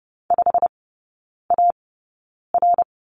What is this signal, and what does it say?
Morse code, heard as a keyed tone, about 30 wpm, spaced as for 9 wpm, 700 Hz. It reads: 5AR
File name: course / 30:10wpm / 087 Lesson 15e - Number 5 - Three Characters 30wpm